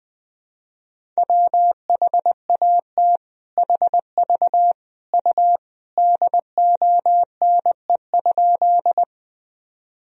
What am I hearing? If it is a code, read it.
Morse code, heard as a keyed tone, about 20 wpm, 700 Hz. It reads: WHAT HV U DONE?
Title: Sentences from Top 500 Words and Abbreviations 20wpm